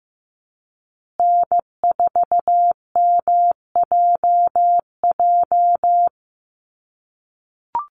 Morse code, heard as a keyed tone, about 15 wpm, 700 Hz. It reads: N4MJJ
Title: USA Call Signs Repeated 3x 15wpm